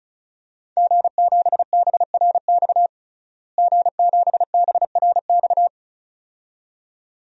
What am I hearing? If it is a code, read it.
Morse code, heard as a keyed tone, about 36 wpm, 700 Hz. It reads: G7BRX G7BRX
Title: DXCC-100 Call Signs - 5 Characters - Repeated 2x - 35wpm